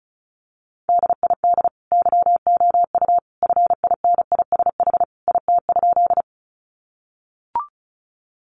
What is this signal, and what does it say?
Morse code, heard as a keyed tone, about 34 wpm, 700 Hz. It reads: DID YOU FINISH IT?
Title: Sentences from Top 1200 Words 35wpm